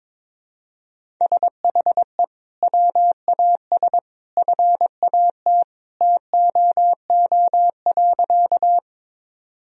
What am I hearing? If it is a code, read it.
Morse code, heard as a keyed tone, about 22 wpm, 700 Hz. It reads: SHE WAS FAT TOO.
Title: Sentences from Top 1000 Words - Review Part 4 - 22wpm